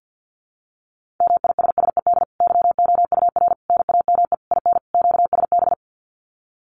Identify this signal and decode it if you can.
Morse code, heard as a keyed tone, about 48 wpm, 700 Hz. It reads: MISSED YOUR NAME IN QSB
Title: Common QSO Phrases 50wpm